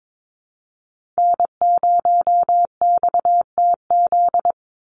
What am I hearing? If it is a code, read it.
Morse code, heard as a keyed tone, about 22 wpm, 700 Hz. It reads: N0XTZ
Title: USA Call Signs 22wpm